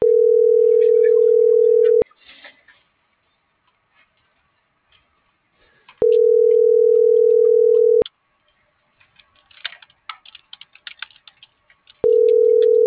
Ringback tone (North America), 440 + 480 Hz.